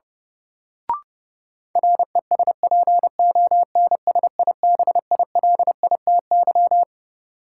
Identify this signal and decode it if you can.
Morse code, heard as a keyed tone, about 30 wpm, 700 Hz. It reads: RESPONSIBILITY
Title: Sentences from Top 1200 Words - New Word Drill - 30wpm